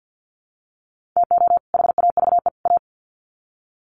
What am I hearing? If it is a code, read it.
Morse code, heard as a keyed tone, about 50 wpm, 700 Hz. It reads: TO HAVE A
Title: Sets of 3 Words 50wpm